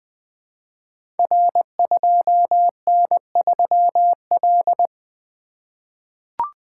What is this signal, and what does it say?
Morse code, heard as a keyed tone, about 20 wpm, 700 Hz. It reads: R2N3L
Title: Code Groups - 5 Letter:Numbers - 20wpm